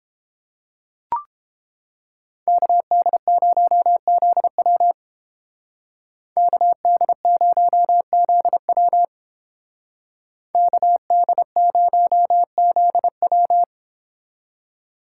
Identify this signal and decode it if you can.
Morse code, heard as a keyed tone, about 29 wpm, 700 Hz. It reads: KD0ZW KD0ZW KD0ZW